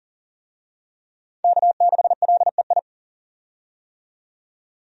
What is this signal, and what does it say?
Morse code, heard as a keyed tone, about 40 wpm, 700 Hz. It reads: K6LEI
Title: USA Call Signs - 5 Characters - 40wpm